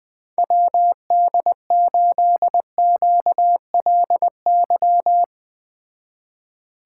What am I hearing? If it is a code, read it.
Morse code, heard as a keyed tone, about 20 wpm, 700 Hz. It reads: WD8QLY
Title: Speed Racing - USA 2x3 Call Signs 17wpm